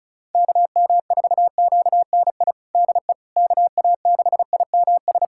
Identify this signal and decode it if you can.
Morse code, heard as a keyed tone, about 34 wpm, 700 Hz. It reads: KM4QNI DE KA6IMS
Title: USA Callsign DE Callsign - 6 Characters - 35wpm